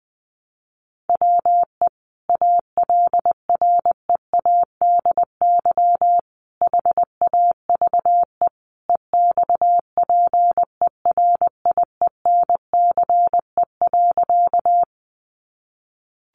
Morse code, heard as a keyed tone, about 19 wpm, 700 Hz. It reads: WE ALREADY HAVE EXPERIENCE.
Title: Sentences from Top 700 Words 20wpm